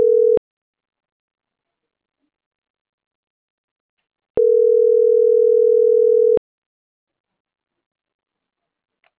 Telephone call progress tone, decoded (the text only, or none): ringback tone